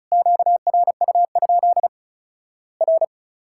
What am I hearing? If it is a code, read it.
Morse code, heard as a keyed tone, about 36 wpm, 700 Hz. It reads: QRU? R
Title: Worked All Europe DX Contest - QTC Exchange - 35wpm